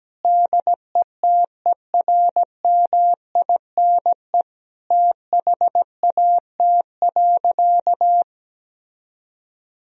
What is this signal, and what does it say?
Morse code, heard as a keyed tone, about 17 wpm, 700 Hz. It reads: DETERMINE THAT.
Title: Sentences from Top 1000 Words - Review Part 4 - 17wpm